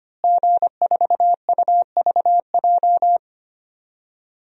Morse code, heard as a keyed tone, about 25 wpm, 700 Hz. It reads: G4UVJ
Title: DXCC-100 Call Signs - 5 Characters - Repeated 3x - 25wpm